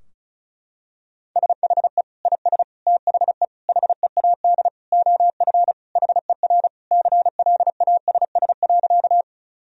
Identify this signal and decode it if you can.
Morse code, heard as a keyed tone, about 36 wpm, 700 Hz. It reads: SHE IS THE HEAD OF HER CLASS.
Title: Sentences from Top 1000 Words - Review Part 2 - 35wpm